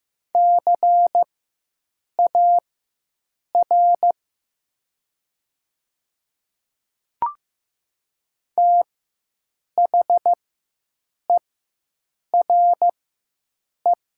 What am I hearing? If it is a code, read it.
Morse code, heard as a keyed tone, about 15 wpm, spaced as for 7 wpm, 700 Hz. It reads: CAR THERE
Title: 4x Character Spacing - Top 500 Words - 15wpm